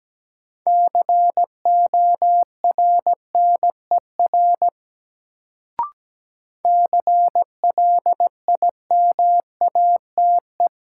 Morse code, heard as a keyed tone, about 17 wpm, 700 Hz. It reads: CORNER CLIMATE
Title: Sentences from Top 1600 Words - New Word Drill - 17wpm